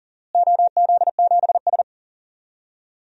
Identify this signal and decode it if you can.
Morse code, heard as a keyed tone, about 40 wpm, 700 Hz. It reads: OZ7S
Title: DXCC-100 Call Signs - 4 Characters - Repeated 2x - 40wpm